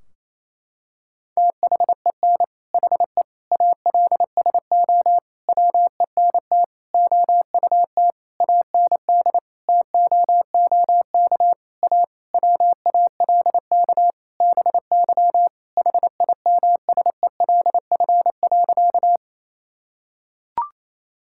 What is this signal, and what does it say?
Morse code, heard as a keyed tone, about 28 wpm, 700 Hz. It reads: THEN HE ALSO WENT OUT AND TOOK A WALK BY HIMSELF.
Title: Sentences from Top 1600 Words 28wpm